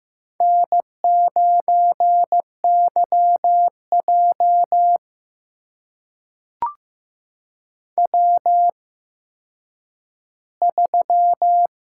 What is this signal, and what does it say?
Morse code, heard as a keyed tone, about 15 wpm, spaced as for 9 wpm, 700 Hz. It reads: N9YJ W3
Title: USA Call Signs - 4 Characters - 8x Character Spacing - 15wpm